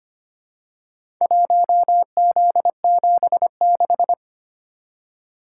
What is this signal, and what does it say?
Morse code, heard as a keyed tone, about 25 wpm, 700 Hz. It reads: 1Z76
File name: practice / 25wpm / Four Letter-Number Combinations - 25wpm